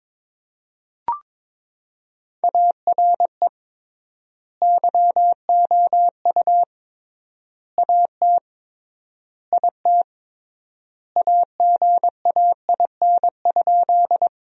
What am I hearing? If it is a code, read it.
Morse code, heard as a keyed tone, about 22 wpm, 700 Hz. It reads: ARE YOU AT IT AGAIN?
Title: 3x Word-Spacing - Sentences from Top 200 Words - 22wpm